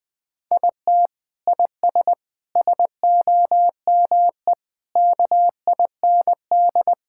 Morse code, heard as a keyed tone, about 21 wpm, 700 Hz. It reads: IT IS SOME KIND
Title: Sentences with 3-Word Phrases - 20wpm